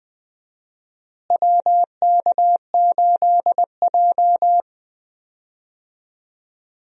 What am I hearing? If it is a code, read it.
Morse code, heard as a keyed tone, about 20 wpm, 700 Hz. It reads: WK8J
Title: USA Call Signs - 4 Characters - Repeated 2x - 20wpm